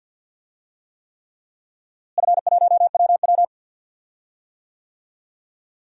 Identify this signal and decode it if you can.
Morse code, heard as a keyed tone, about 52 wpm, 700 Hz. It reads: F1WW
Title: DXCC-100 Call Signs - 4 Characters - Repeated 3x - 50wpm